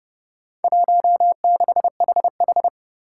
Morse code, heard as a keyed tone, about 30 wpm, 700 Hz. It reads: 16HH
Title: Four Challenging Letter-Number Combinations 30wpm